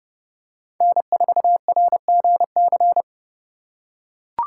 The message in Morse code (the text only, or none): N4RGC